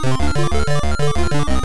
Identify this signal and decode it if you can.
There is no code.